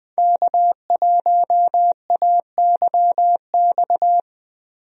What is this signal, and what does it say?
Morse code, heard as a keyed tone, about 20 wpm, 700 Hz. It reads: K1AYX
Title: USA Call Signs - 5 Characters - Repeated 3x - 20wpm